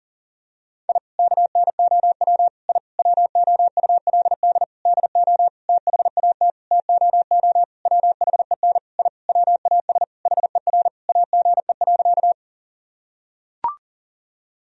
Morse code, heard as a keyed tone, about 40 wpm, 700 Hz. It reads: I KNOW I WOULD DO THAT TOO WHEN I WAS HER AGE.
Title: Sentences from Top 1000 Words - Review Part 3 - 40wpm